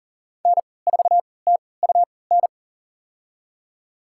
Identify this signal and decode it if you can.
Morse code, heard as a keyed tone, about 40 wpm, spaced as for 23 wpm, 700 Hz. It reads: N4TUN
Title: USA Call Signs - 5 Characters - 3x Character Spacing - 40wpm